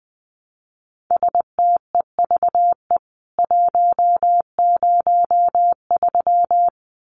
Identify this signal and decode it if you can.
Morse code, heard as a keyed tone, about 20 wpm, 700 Hz. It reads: STEVE 103
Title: CWT Exchange 20wpm